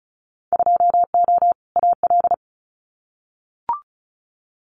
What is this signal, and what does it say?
Morse code, heard as a keyed tone, about 36 wpm, 700 Hz. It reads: 2O AL
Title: Winter Field Day Exchanges Encore 35wpm